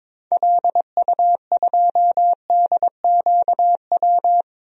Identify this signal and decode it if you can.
Morse code, heard as a keyed tone, about 22 wpm, 700 Hz. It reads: LU2DQW
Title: DXCC-100 Call Signs - 6 Characters - Repeated 3x - 22wpm